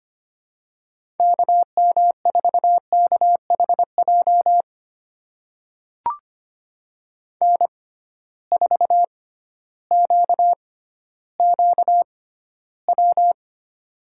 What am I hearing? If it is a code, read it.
Morse code, heard as a keyed tone, about 25 wpm, spaced as for 13 wpm, 700 Hz. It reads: KM4KHJ N4QQW
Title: USA Call Signs - 6x Character Spacing - 25wpm